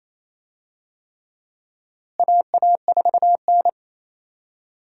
Morse code, heard as a keyed tone, about 28 wpm, 700 Hz. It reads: AA4N
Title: USA Call Signs - 4 Characters - Repeated 3x - 28wpm